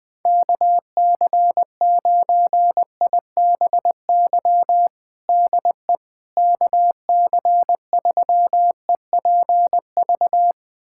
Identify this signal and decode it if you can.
Morse code, heard as a keyed tone, about 20 wpm, 700 Hz. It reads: KC9IBY DE KC3EPV
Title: USA Callsign DE Callsign - 6 Characters - 20wpm